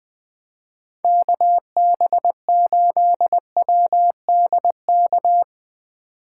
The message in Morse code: KB8WDK

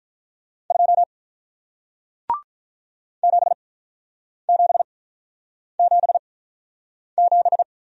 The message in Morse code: 2 7777

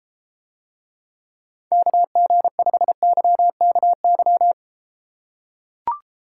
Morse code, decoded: KG5YKY

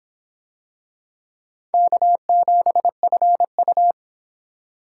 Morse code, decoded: K7FU

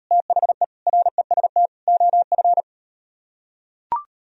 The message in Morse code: THE REST OF

